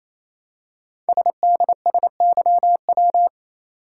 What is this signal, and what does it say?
Morse code, heard as a keyed tone, about 27 wpm, 700 Hz. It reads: SDSYW
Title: Code Groups - 5 Letter:Numbers - 28wpm